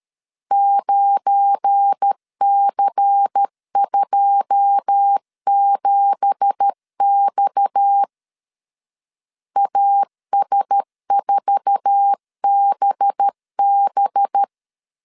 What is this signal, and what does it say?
Morse code, heard as a keyed tone, about 13 wpm, 800 Hz. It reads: 9C27X AS4BB
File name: CW / DET / 1 9 1500C S B 12wpm 3819